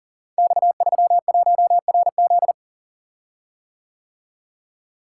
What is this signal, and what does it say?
Morse code, heard as a keyed tone, about 42 wpm, 700 Hz. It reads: X31RZ
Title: Code Groups - 5 Letter:Numbers - 40wpm